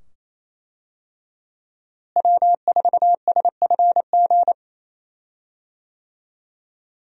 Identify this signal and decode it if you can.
Morse code, heard as a keyed tone, about 28 wpm, 700 Hz. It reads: W4SFG
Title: USA Call Signs - 5 Characters - Repeated 2x - 28wpm